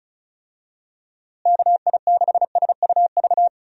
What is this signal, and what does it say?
Morse code, heard as a keyed tone, about 34 wpm, 700 Hz. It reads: KI6SUV
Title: USA Call Signs - 6 Characters - Repeated 2x - 35wpm